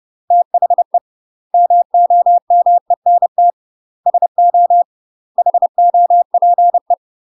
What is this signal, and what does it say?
Morse code, heard as a keyed tone, about 30 wpm, 700 Hz. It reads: THE MOMENT SO HOPE
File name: UK MASTER QSOs TRAINING / 30WPM / UK MASTER QSOs 0018